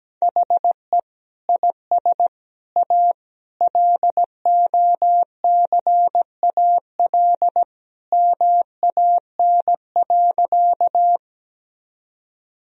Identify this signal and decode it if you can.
Morse code, heard as a keyed tone, about 17 wpm, 700 Hz. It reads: HE IS A LOCAL MAN.
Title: Sentences from Top 1000 Words - Review Part 3 - 17wpm